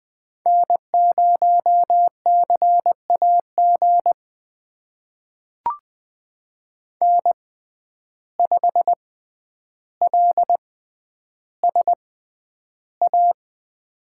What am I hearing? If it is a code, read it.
Morse code, heard as a keyed tone, about 21 wpm, spaced as for 10 wpm, 700 Hz. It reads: N0CAG N5LSA
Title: USA Call Signs - 5 Characters - 6x Character Spacing - 20wpm